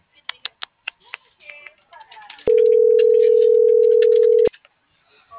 Ringback tone (North America), 440 + 480 Hz.